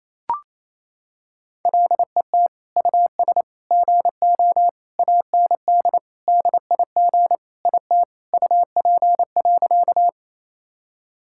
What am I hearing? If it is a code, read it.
Morse code, heard as a keyed tone, about 28 wpm, 700 Hz. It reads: LET US GO AND DIG IT UP.
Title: Sentences from Top 1600 Words 28wpm